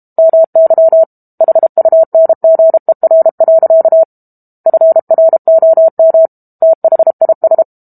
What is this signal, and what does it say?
Morse code, heard as a keyed tone, about 32 wpm, 650 Hz. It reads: MY HUNGER. FROM THIS
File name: Book Two / 06-Chapter 6 - 32 WPM